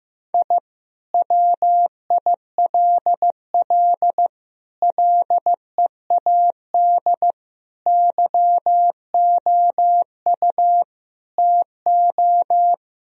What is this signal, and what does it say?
Morse code, heard as a keyed tone, about 15 wpm, 700 Hz. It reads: I WILL LEAD YOU TO
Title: Sentences from Top 1000 Words - Review Part 2 - 15wpm